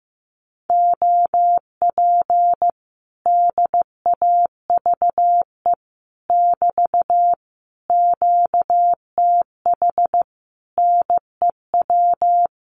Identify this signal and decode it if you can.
Morse code, heard as a keyed tone, about 15 wpm, 700 Hz. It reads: OP DAVE <BT> QTH NEW
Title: Common QSO Phrases Encore 15wpm